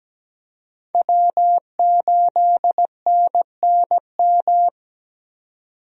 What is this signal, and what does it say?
Morse code, heard as a keyed tone, about 17 wpm, 700 Hz. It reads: W8NNM
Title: USA Call Signs - 5 Characters - 17wpm